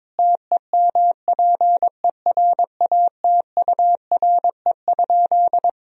Morse code, heard as a keyed tone, about 21 wpm, 700 Hz. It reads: TEMPERATURE?
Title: Sentences from Top 1000 Words - Review Part 4 - 22wpm